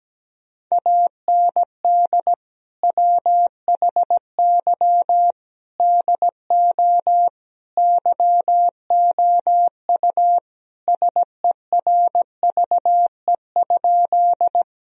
Morse code, heard as a keyed tone, about 17 wpm, 700 Hz. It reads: AND WHY DO YOU SERVE?
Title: Sentences from Top 1000 Words - Review Part 2 - 17wpm